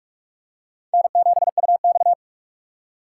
Morse code, heard as a keyed tone, about 46 wpm, 700 Hz. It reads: N7UX
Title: USA Call Signs - 3x Character Spacing - 45wpm